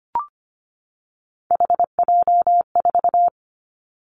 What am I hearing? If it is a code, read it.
Morse code, heard as a keyed tone, about 25 wpm, 700 Hz. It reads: HJ4
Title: Callsign Prefixes for Top 100 DXCC - 25wpm